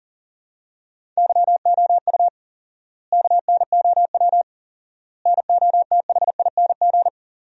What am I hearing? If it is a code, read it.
Morse code, heard as a keyed tone, about 40 wpm, 700 Hz. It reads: YOU KNOW NOTHING
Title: 4x Word-Spacing - Sentences from Top 500 Words - 40wpm